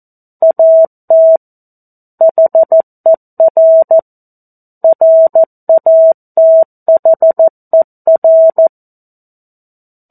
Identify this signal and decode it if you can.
Morse code, heard as a keyed tone, about 14 wpm, 650 Hz. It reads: AT HER RATHER
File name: 02-Chapter-02 14 WPM